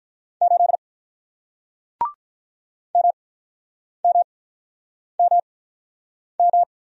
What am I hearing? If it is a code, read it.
Morse code, heard as a keyed tone, about 47 wpm, spaced as for 9 wpm, 700 Hz. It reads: 8 MMMM